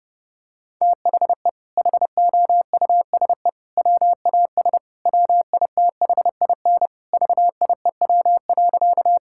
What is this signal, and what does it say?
Morse code, heard as a keyed tone, about 30 wpm, 700 Hz. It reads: THE HOUSE WAS WITHIN VIEW.
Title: Sentences from Top 1000 Words - Review Part 6 - 30wpm